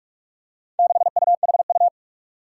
Morse code, heard as a keyed tone, about 44 wpm, 700 Hz. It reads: 6UHU
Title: Four Challenging Letter-Number Combinations 45wpm